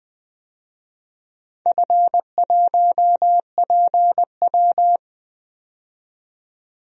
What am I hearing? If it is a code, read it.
Morse code, heard as a keyed tone, about 20 wpm, 700 Hz. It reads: F1PW
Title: DXCC-100 Call Signs - 4 Characters - 20wpm